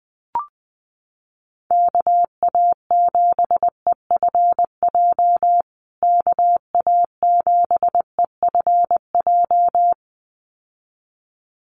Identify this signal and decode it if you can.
Morse code, heard as a keyed tone, about 20 wpm, 700 Hz. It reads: KA7EFJ KA7EFJ
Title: USA Call Signs Repeated 2x 20wpm